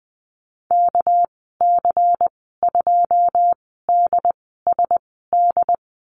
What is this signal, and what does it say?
Morse code, heard as a keyed tone, about 20 wpm, spaced as for 14 wpm, 700 Hz. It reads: KC2DSD
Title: 200 Lesson 32g - Letter k - Call Signs 20wpm